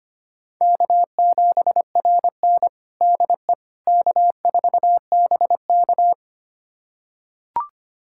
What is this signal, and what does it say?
Morse code, heard as a keyed tone, about 25 wpm, 700 Hz. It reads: K7RN DE K4BK
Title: USA Callsign DE Callsign - 4 Characters - 25wpm